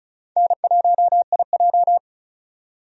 Morse code, heard as a keyed tone, about 36 wpm, 700 Hz. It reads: N1IJ
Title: USA Call Signs - 4 Characters - 35wpm